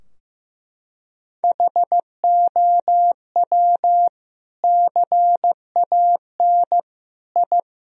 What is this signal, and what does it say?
Morse code, heard as a keyed tone, about 15 wpm, 700 Hz. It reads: HOW CAN I
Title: Sentences from Top 1000 Words - Review Part 3 - 15wpm